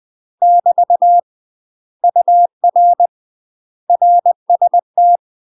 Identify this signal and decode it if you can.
Morse code, heard as a keyed tone, about 20 wpm, 700 Hz. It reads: <BT> UR RST
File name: UK MASTER QSOs TRAINING / 20WPM / UK MASTER QSOs 0055